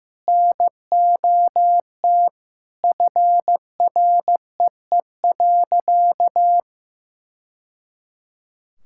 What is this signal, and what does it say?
Morse code, heard as a keyed tone, about 15 wpm, 700 Hz. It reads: NOT FREE.
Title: Sentences from Top 1000 Words - Review Part 5 - 15wpm